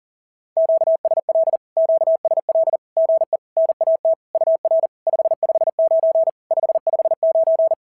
Morse code, heard as a keyed tone, about 42 wpm, 650 Hz. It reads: QSL QSL GE NAT UR 559 559
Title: SOTA QSO Practice 40wpm